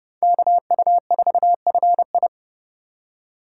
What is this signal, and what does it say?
Morse code, heard as a keyed tone, about 38 wpm, 700 Hz. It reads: KU4FI